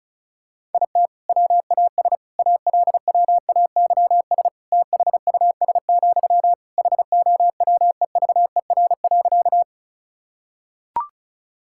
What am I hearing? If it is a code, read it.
Morse code, heard as a keyed tone, about 36 wpm, 700 Hz. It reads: IT WAS ALWAYS THUS, HOWEVER.
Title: Sentences from Top 1000 Words - Review Part 5 - 35wpm